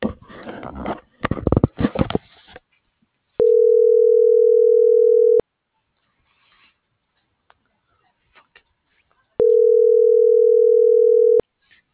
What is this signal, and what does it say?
Ringback tone (North America), 440 + 480 Hz.